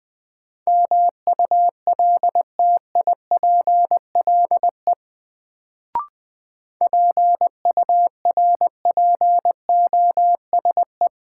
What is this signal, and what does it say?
Morse code, heard as a keyed tone, about 20 wpm, 700 Hz. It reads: MULTIPLE PURPOSE
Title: Sets of 5 Words - New Word Drill - 20wpm